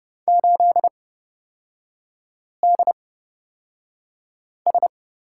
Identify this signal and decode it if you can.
Morse code, heard as a keyed tone, about 29 wpm, spaced as for 5 wpm, 700 Hz. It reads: 8DS
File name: Lesson 37e - Letter x - Three Characters 30wpm